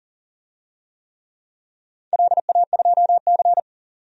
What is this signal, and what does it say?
Morse code, heard as a keyed tone, about 40 wpm, 700 Hz. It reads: LA2C
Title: DXCC-100 Call Signs - 4 Characters - 40wpm